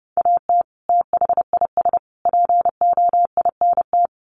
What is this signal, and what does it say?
Morse code, heard as a keyed tone, about 30 wpm, 700 Hz. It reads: AT THIS POINT
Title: Sets of 3 Words 30wpm